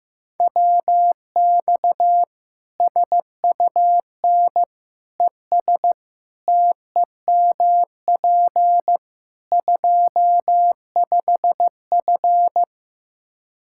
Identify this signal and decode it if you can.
Morse code, heard as a keyed tone, about 15 wpm, 700 Hz. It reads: WX SUN ES TEMP 25F
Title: Practice QSO - Exchange 2 - Info - 15wpm